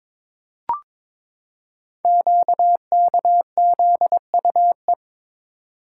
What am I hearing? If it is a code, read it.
Morse code, heard as a keyed tone, about 22 wpm, 700 Hz. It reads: QKZUE
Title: Code Groups - 5 Letters - 22wpm